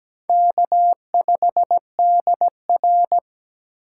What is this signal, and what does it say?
Morse code, heard as a keyed tone, about 17 wpm, 700 Hz. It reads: K5DR
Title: USA Call Signs - 4 Characters - 4x Character Spacing - 17wpm